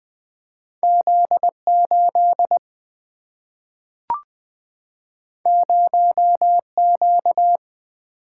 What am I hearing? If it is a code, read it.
Morse code, heard as a keyed tone, about 20 wpm, 700 Hz. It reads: Z 8 0 Q
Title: Single-Letter Single-Number 20wpm